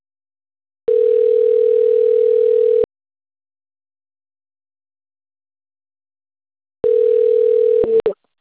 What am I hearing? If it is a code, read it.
Ringback tone (North America), 440 + 480 Hz.